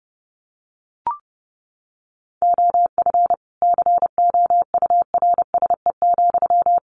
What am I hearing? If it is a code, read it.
Morse code, heard as a keyed tone, about 30 wpm, 700 Hz. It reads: OF COURSE,
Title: Sentences from Top 1500 Words 30wpm